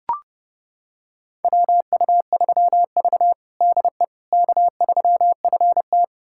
Morse code, heard as a keyed tone, about 30 wpm, 700 Hz. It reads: WU3V DE K3FT
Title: USA Callsign DE Callsign - 4 Characters - 30wpm